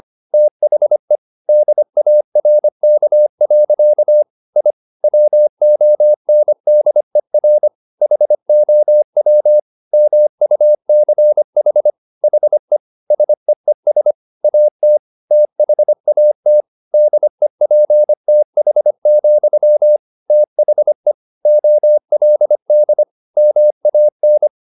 Morse code, heard as a keyed tone, about 25 wpm, 600 Hz. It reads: THE DARK. I WONDER HOW MUCH HE SEES AT THAT DEPTH, THE OLD MAN
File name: the old man and the sea part 1 25wpm